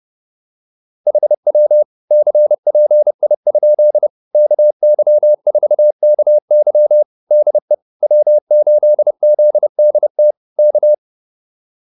Morse code, heard as a keyed tone, about 30 wpm, 600 Hz. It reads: HW CPI? KY4KY DE W8ZDT K
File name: QSO108 30